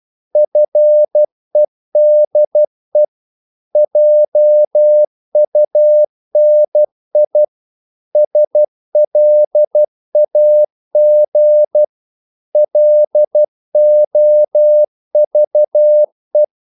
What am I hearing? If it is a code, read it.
Morse code, heard as a keyed tone, about 12 wpm, 600 Hz. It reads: FEDE JUNI SLAG LOVE
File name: Korte ord DK 12wpm